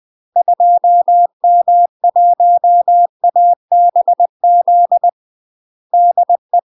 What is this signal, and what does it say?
Morse code, heard as a keyed tone, about 20 wpm, 700 Hz. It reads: 2M1ABZ DE
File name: UK MASTER QSOs TRAINING / 20WPM / UK MASTER QSOs 0004